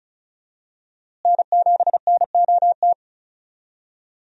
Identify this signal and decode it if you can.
Morse code, heard as a keyed tone, about 34 wpm, 700 Hz. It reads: N7NOT
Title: USA Call Signs - 5 Characters - Repeated 3x - 35wpm